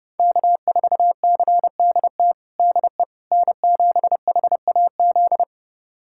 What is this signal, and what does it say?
Morse code, heard as a keyed tone, about 32 wpm, 700 Hz. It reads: K4CDT DE N7HAZ
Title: USA Callsign DE Callsign - 5 Characters - 30wpm